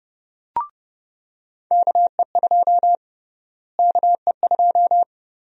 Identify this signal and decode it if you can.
Morse code, heard as a keyed tone, about 30 wpm, 700 Hz. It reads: KE2 KE2
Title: USA Call Sign Prefixes - Repeated 2x - 30wpm